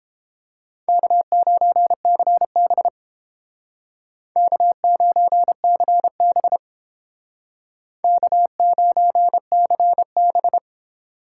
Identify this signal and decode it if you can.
Morse code, heard as a keyed tone, about 29 wpm, 700 Hz. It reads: K9CB K9CB K9CB